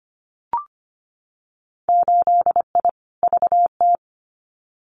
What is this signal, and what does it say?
Morse code, heard as a keyed tone, about 25 wpm, 700 Hz. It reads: 8I VT
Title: Winter Field Day Exchanges Encore 25wpm